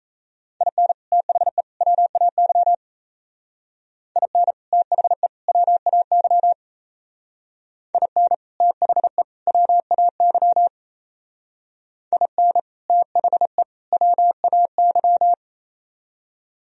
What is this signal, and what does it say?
Morse code, heard as a keyed tone, about 37 wpm, 700 Hz. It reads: IN THE WAY IN THE WAY IN THE WAY IN THE WAY